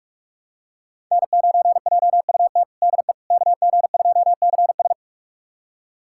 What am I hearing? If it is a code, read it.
Morse code, heard as a keyed tone, about 46 wpm, 700 Hz. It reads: N9JUT DE KG2CS
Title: USA Callsign DE Callsign - 5 Characters - 45wpm